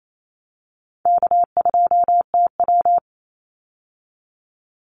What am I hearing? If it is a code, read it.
Morse code, heard as a keyed tone, about 28 wpm, 700 Hz. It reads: K2TW
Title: Easy Call Signs - 28wpm